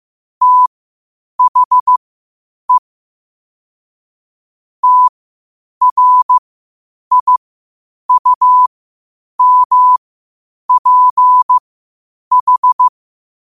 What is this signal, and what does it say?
Morse code, heard as a keyed tone, about 15 wpm, spaced as for 8 wpm, 1000 Hz. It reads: THE TRIUMPH
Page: Quotes for Fri, 15 Aug 2025 in Morse Code at 8 words per minute.